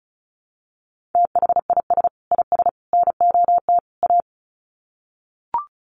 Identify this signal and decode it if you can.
Morse code, heard as a keyed tone, about 34 wpm, 700 Hz. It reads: THIS IS NOT A
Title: Sets of 4 Words 35wpm